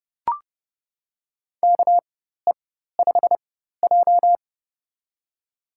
Morse code, heard as a keyed tone, about 29 wpm, spaced as for 14 wpm, 700 Hz. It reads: KE5J
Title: USA Call Signs - 4x Character Spacing - 30wpm